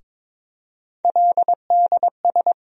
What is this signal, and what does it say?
Morse code, heard as a keyed tone, about 22 wpm, 700 Hz. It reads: LDS
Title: USA Call Sign Postfixes - 3 Letters 22wpm